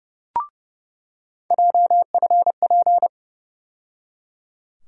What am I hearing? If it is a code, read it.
Morse code, heard as a keyed tone, about 30 wpm, 700 Hz. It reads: JFP
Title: USA Call Sign Postfixes - 3 Letters 30wpm